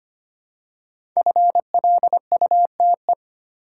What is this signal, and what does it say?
Morse code, heard as a keyed tone, about 25 wpm, 690 Hz. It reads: FLUTE
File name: practice / 25wpm / Easy Words with Hints - 25wpm